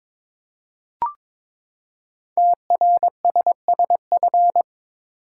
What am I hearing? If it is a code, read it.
Morse code, heard as a keyed tone, about 22 wpm, 700 Hz. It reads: TRSSF
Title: Code Groups - 5 Letters - 22wpm